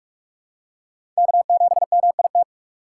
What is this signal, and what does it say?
Morse code, heard as a keyed tone, about 46 wpm, 700 Hz. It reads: K7MIT
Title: USA Call Signs - 5 Characters - 4x Character Spacing - 45wpm